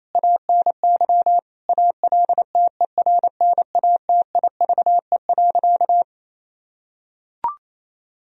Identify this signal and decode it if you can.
Morse code, heard as a keyed tone, about 28 wpm, 700 Hz. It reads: ANY ALTERNATIVE.
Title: Sentences from Top 1000 Words - Review Part 4 - 28wpm